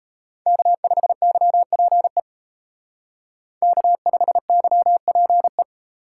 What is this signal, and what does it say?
Morse code, heard as a keyed tone, about 32 wpm, 700 Hz. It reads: K5YPE K5YPE
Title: Speed Racing - USA 1x3 Call Signs 28wpm